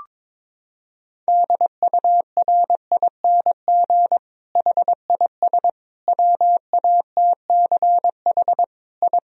Morse code, heard as a keyed tone, about 22 wpm, 700 Hz. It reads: DURING HIS WATCH I
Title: Sentences from Top 1000 Words - Review Part 2 - 22wpm